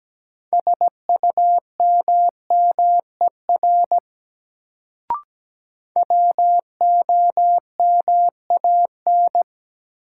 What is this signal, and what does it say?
Morse code, heard as a keyed tone, about 17 wpm, 700 Hz. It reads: SUMMER WOMAN
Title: Sentences from Top 1100 Words - New Word Drill Alt - 17wpm